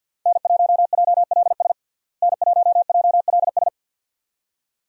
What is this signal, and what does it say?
Morse code, heard as a keyed tone, about 47 wpm, 700 Hz. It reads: N1JLS N1JLS
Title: USA Call Signs - 5 Characters - Repeated 2x - 50wpm